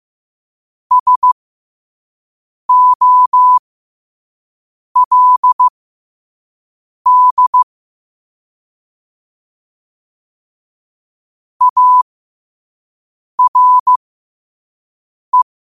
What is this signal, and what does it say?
Morse code, heard as a keyed tone, about 15 wpm, spaced as for 5 wpm, 1000 Hz. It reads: SOLD ARE
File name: qotdfiles / 5wpm